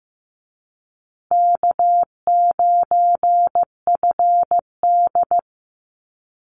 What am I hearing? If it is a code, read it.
Morse code, heard as a keyed tone, about 15 wpm, 700 Hz. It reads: K9FD
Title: Easy Call Signs Repeated 2x - 15wpm